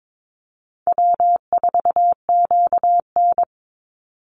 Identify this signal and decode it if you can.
Morse code, heard as a keyed tone, about 22 wpm, 700 Hz. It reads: W4QN
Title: Easy Call Signs Repeated 3x - 22wpm